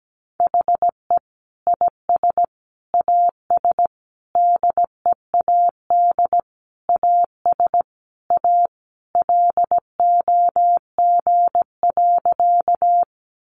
Morse code, heard as a keyed tone, about 17 wpm, 700 Hz. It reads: HE IS AS DEAD AS A LOG.
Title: Sentences from Top 1500 Words 17wpm